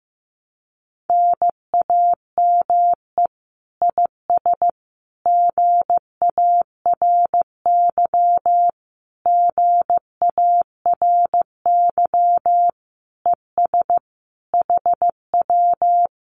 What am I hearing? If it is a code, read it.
Morse code, heard as a keyed tone, about 15 wpm, 700 Hz. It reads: NAME IS GARY GARY ES HW
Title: Common QSO Phrases Encore 15wpm